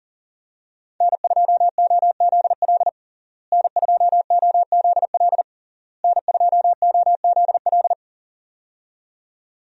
Morse code, heard as a keyed tone, about 40 wpm, 700 Hz. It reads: N2OZL N2OZL N2OZL
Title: USA Call Signs - 5 Characters - Repeated 3x - 40wpm